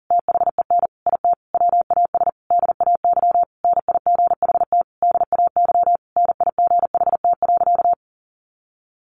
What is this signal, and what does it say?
Morse code, heard as a keyed tone, about 40 wpm, 700 Hz. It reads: THEN IT WAS DAY NIGHT DAY NIGHT.
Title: A Sound Of Thunder 40wpm